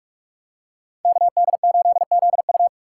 Morse code, heard as a keyed tone, about 44 wpm, 700 Hz. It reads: KD8ZU